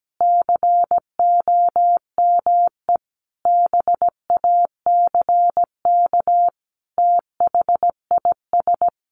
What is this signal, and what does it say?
Morse code, heard as a keyed tone, about 17 wpm, 700 Hz. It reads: COME BACK THIS
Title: A Sound Of Thunder 17wpm